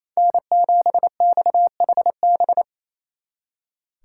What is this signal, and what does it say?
Morse code, heard as a keyed tone, about 36 wpm, 700 Hz. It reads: N7XHB